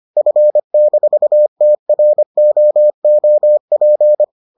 Morse code, heard as a keyed tone, about 25 wpm, 600 Hz. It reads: F-TROOP